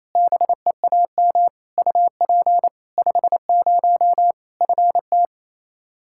Morse code, heard as a keyed tone, about 28 wpm, 700 Hz. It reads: BEAM UP 50 FT
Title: Practice QSO - Exchange 2 - Info - 28wpm